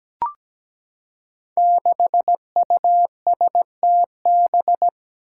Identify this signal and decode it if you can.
Morse code, heard as a keyed tone, about 17 wpm, 700 Hz. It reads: 6USTB
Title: Five Challenging Letter-Number Combinations 17wpm